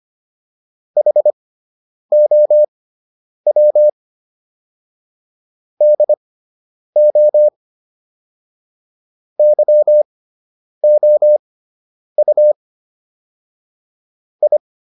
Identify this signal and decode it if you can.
Morse code, heard as a keyed tone, about 25 wpm, spaced as for 9 wpm, 600 Hz. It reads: HOW DO YOU I